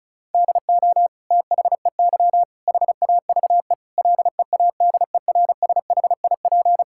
Morse code, heard as a keyed tone, about 34 wpm, 700 Hz. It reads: DO THEY HAVE LEADERSHIP
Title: Sentences from Top 1600 Words 35wpm